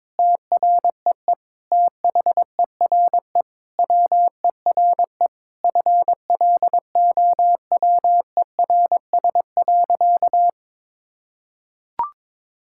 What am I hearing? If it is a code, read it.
Morse code, heard as a keyed tone, about 22 wpm, 700 Hz. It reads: TREE THERE WERE FLOWERS.
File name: Sentences with 9-Word Phrases - 22wpm